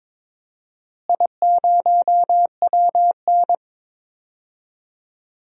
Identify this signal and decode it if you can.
Morse code, heard as a keyed tone, about 22 wpm, 700 Hz. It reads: I0WN
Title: DXCC-100 Call Signs - 4 Characters - Repeated 3x - 22wpm